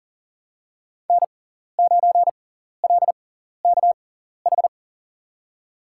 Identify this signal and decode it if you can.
Morse code, heard as a keyed tone, about 40 wpm, spaced as for 14 wpm, 700 Hz. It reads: N9LKH
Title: USA Call Signs - 5 Characters - 6x Character Spacing - 40wpm